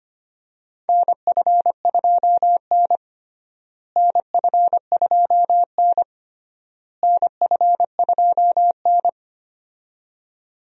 Morse code, heard as a keyed tone, about 25 wpm, 700 Hz. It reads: NF2N NF2N NF2N
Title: USA Call Signs - 4 Characters - Repeated 3x - 25wpm